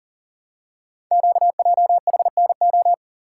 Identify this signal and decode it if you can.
Morse code, heard as a keyed tone, about 43 wpm, 700 Hz. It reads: QJHNO
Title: Code Groups - 5 Letters - 40wpm